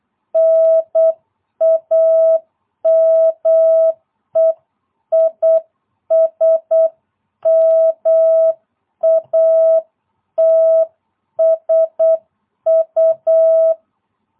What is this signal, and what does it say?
Morse code, heard as a keyed tone, about 8 wpm, 650 Hz. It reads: NAME IS MATSU